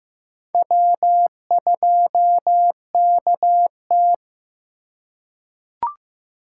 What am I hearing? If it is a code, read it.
Morse code, heard as a keyed tone, about 15 wpm, 700 Hz. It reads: W2KT
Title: USA Call Signs - 4 Characters - 4x Character Spacing - 15wpm